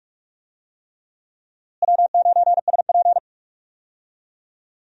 Morse code, heard as a keyed tone, about 41 wpm, 700 Hz. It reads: W9SP
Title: USA Call Signs - 4 Characters - Repeated 3x - 45wpm